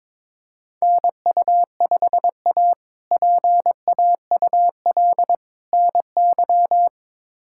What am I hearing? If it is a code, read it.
Morse code, heard as a keyed tone, about 22 wpm, 700 Hz. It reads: NU5A PAUL NY
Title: K1USN SST QSO Practice - 22wpm